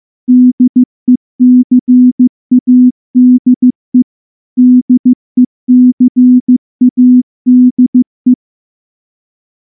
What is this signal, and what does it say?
Morse code, heard as a keyed tone, about 14 wpm, 250 Hz. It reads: DECADE DECADE
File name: morse-training-15WPM